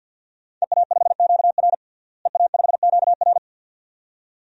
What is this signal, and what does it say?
Morse code, heard as a keyed tone, about 50 wpm, 700 Hz. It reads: EA5QR EA5QR
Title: DXCC-100 Call Signs - 5 Characters - Repeated 2x - 50wpm